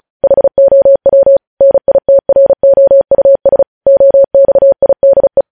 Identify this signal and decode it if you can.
Morse code, heard as a keyed tone, about 36 wpm, 550 Hz. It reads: HOW NITROUS OXIDE
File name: podcast / News headlines-1x-35x35